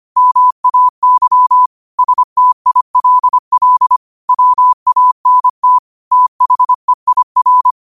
Morse code, heard as a keyed tone, about 25 wpm, 1000 Hz. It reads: MAY STILL WANT THEIR
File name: qotdfiles / 25wpm